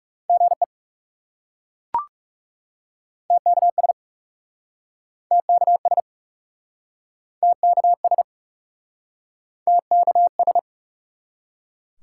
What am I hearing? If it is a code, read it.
Morse code, heard as a keyed tone, about 37 wpm, 700 Hz. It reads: GE TKS TKS TKS TKS
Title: Speed-Racing Top 90 QSO Elements 30wpm